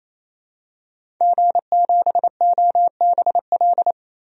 Morse code, heard as a keyed tone, about 28 wpm, 700 Hz. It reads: G7OBL
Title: DXCC-100 Call Signs - 5 Characters - Repeated 3x - 28wpm